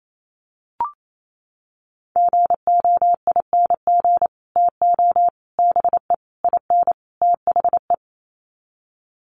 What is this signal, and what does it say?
Morse code, heard as a keyed tone, about 28 wpm, 700 Hz. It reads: GOING TO BE IN THE
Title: Sets of 5 Words Encore 28wpm